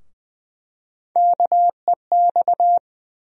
Morse code, heard as a keyed tone, about 20 wpm, 700 Hz. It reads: KEX
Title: USA Call Sign Postfixes - 3 Letters 20wpm